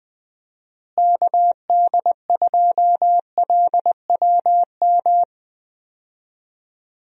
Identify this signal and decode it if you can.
Morse code, heard as a keyed tone, about 20 wpm, 700 Hz. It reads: KD2LWM
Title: USA Call Signs - 6 Characters - Repeated 3x - 20wpm